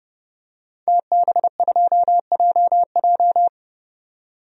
Morse code, heard as a keyed tone, about 31 wpm, 700 Hz. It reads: TB2JJ
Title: DXCC-100 Call Signs - 5 Characters - 30wpm